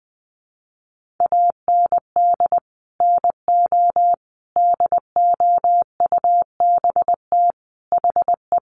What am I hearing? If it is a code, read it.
Morse code, heard as a keyed tone, about 20 wpm, 700 Hz. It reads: AND NO DOUBT HE
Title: Sentences from Top 1500 Words 20wpm